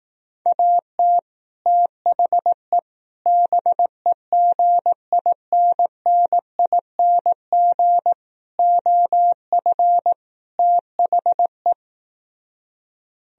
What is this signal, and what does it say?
Morse code, heard as a keyed tone, about 18 wpm, 700 Hz. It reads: AT THE BEGINNING OF THE
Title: Speed-Racing - Sets of 5 Words 15wpm